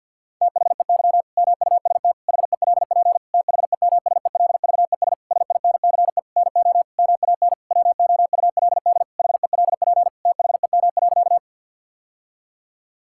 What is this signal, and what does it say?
Morse code, heard as a keyed tone, about 50 wpm, 700 Hz. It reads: THEY MUST HELP THEMSELVES SINCE NO MAN WOULD HELP THEM.
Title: Sentences from Top 1000 Words - Review Part 5 - 50wpm